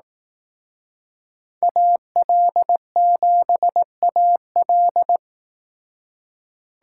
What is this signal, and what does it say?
Morse code, heard as a keyed tone, about 18 wpm, 700 Hz. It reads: AL7AL
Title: Speed Racing - USA 2x2 Call Signs 15wpm